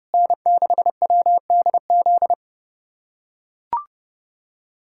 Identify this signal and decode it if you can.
Morse code, heard as a keyed tone, about 31 wpm, 700 Hz. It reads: N6WDZ
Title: USA Callsign DE Callsign - 5 Characters - 30wpm